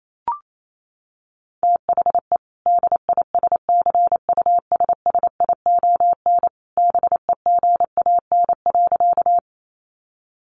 Morse code, heard as a keyed tone, about 28 wpm, 700 Hz. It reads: THE DISCUSSION BEGAN.
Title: Sentences from Top 1100 Words 28wpm